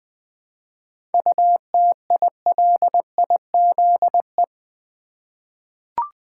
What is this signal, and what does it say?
Morse code, heard as a keyed tone, about 20 wpm, 690 Hz. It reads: UTILIZE
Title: 7 Letter Words - First 3 Letters Given - 20wpm